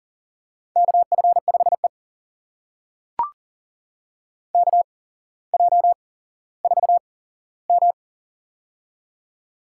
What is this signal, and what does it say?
Morse code, heard as a keyed tone, about 39 wpm, spaced as for 17 wpm, 700 Hz. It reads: KF5E KJ4M
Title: USA Call Signs - 4 Characters - 8x Character Spacing - 40wpm